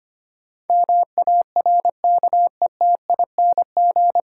Morse code, heard as a keyed tone, about 25 wpm, 700 Hz. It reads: MARKETING
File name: practice / 25wpm / 9 Letter Words - First 3 Letters Given - 25wpm